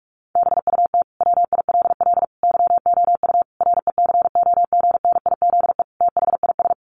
Morse code, heard as a keyed tone, about 44 wpm, 700 Hz. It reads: BUT WILL YOU RECOGNIZE THIS
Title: Sentences from Top 1400 Words 45wpm